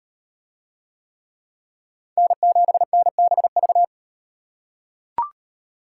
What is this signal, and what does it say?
Morse code, heard as a keyed tone, about 38 wpm, 700 Hz. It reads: N7NBV